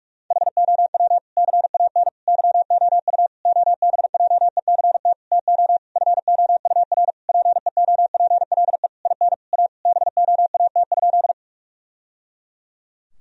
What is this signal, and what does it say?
Morse code, heard as a keyed tone, about 46 wpm, 700 Hz. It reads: HOW CAN YOU OBJECT TO FOUR PEOPLE IN A BOAT?
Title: Sentences from Top 1000 Words - Review Part 3 - 45wpm